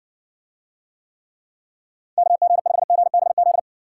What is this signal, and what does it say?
Morse code, heard as a keyed tone, about 55 wpm, 700 Hz. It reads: KG5GBZ